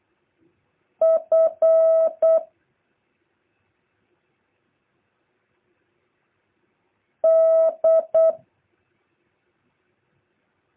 Morse code, heard as a keyed tone, about 8 wpm, 650 Hz. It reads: F D
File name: jh8cbh / 2023cw